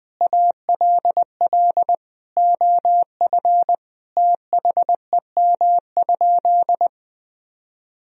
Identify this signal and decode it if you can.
Morse code, heard as a keyed tone, about 20 wpm, 700 Hz. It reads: ALL OF THEM?
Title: Sentences from Top 1000 Words - Review Part 4 - 20wpm